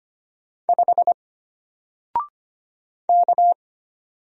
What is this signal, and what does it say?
Morse code, heard as a keyed tone, about 25 wpm, 700 Hz. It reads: 5 K
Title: Single Letter-Number Alt 25wpm